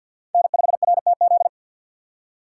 Morse code, heard as a keyed tone, about 47 wpm, 700 Hz. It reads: N5RTZ
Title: USA Call Signs - 5 Characters - 2x Character Spacing - 50wpm